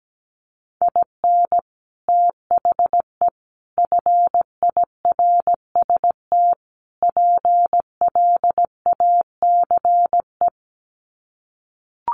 Morse code, heard as a keyed tone, about 17 wpm, 700 Hz. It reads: IN THE FIRST PLACE
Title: Sets of 4 Words 17wpm